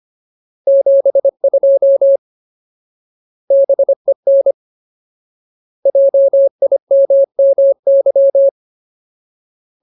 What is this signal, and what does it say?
Morse code, heard as a keyed tone, about 25 wpm, 550 Hz. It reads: 72 BEN JIMMY
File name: Ultimate Word List / 25WPM Ultimate